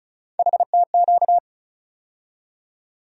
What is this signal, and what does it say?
Morse code, heard as a keyed tone, about 34 wpm, 700 Hz. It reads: HTQ
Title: USA Call Sign Postfixes - 3 Letters 35wpm